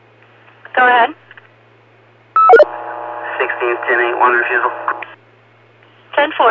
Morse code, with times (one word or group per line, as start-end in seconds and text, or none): none